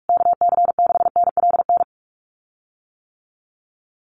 Morse code, heard as a keyed tone, about 44 wpm, 690 Hz. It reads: KC6NLN
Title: USA Call Signs Repeated 3x 45wpm